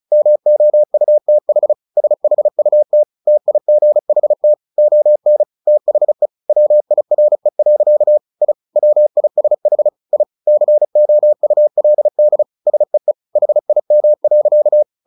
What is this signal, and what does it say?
Morse code, heard as a keyed tone, about 34 wpm, 600 Hz. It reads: MOUTH SHUT TIGHT ON THE WIRE. I WISH I COULD SEE HIM.
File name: the old man and the sea / the old man and the sea 1 35wpm